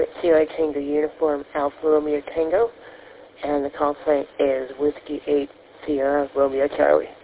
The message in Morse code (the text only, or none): none